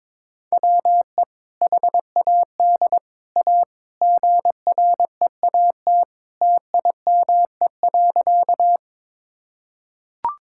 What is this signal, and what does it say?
Morse code, heard as a keyed tone, about 21 wpm, 700 Hz. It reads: WE HAD A GREAT TIME.
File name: practice / 22wpm / Sentences with 2-Word Phrases - 22wpm